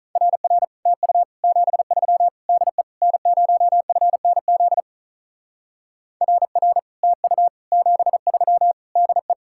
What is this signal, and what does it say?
Morse code, heard as a keyed tone, about 44 wpm, 700 Hz. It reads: RR TU 73 DE N0FNZ RR TU 73 DE